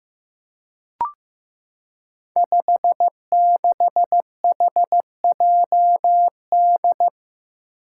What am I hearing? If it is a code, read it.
Morse code, heard as a keyed tone, about 15 wpm, 700 Hz. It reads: 56HJD
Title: Five Challenging Letter-Number Combinations 15wpm